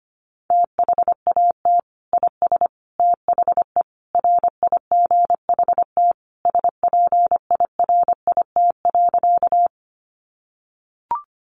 Morse code, heard as a keyed tone, about 25 wpm, 700 Hz. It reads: THAT IS THE RIGHT SPIRIT.
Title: Sentences from Top 1400 Words 25wpm